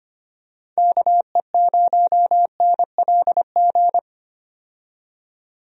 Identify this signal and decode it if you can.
Morse code, heard as a keyed tone, about 25 wpm, 700 Hz. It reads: KE0NLG
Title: USA Call Signs - 6 Characters - 25wpm